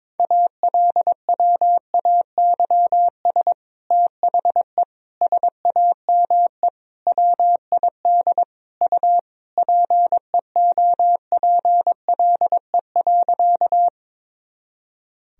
Morse code, heard as a keyed tone, about 21 wpm, 700 Hz. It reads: ALWAYS THE SAME WID U PEOPLE.
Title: Sentences from Top 300 Words and Abbreviations 22wpm